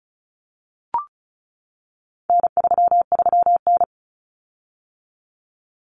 Morse code, heard as a keyed tone, about 34 wpm, 700 Hz. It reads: N33N
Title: Common Cut Numbers - 4-Digit 35wpm